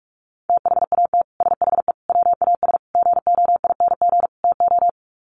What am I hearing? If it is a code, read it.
Morse code, heard as a keyed tone, about 44 wpm, 700 Hz. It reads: THAT SHE WAS GOING TO
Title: Sets of 5 Words Encore2 45wpm